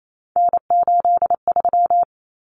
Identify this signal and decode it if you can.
Morse code, heard as a keyed tone, about 28 wpm, 700 Hz. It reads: N83
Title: Common Cut Numbers - 3-Digit 28wpm